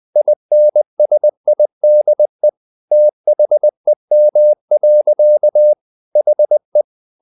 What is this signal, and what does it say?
Morse code, heard as a keyed tone, about 20 wpm, 600 Hz. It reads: INSIDE THEM. HE
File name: the old man and the sea / the old man and the sea part 1 20wpm